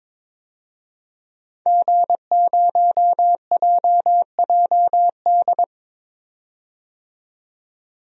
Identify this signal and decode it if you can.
Morse code, heard as a keyed tone, about 22 wpm, 700 Hz. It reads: G0JJD
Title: DXCC-100 Call Signs - 5 Characters - 22wpm